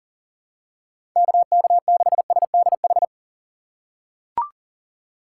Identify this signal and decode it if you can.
Morse code, heard as a keyed tone, about 37 wpm, 700 Hz. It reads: KK6SDH
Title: USA Call Signs - 4x Character Spacing - 40wpm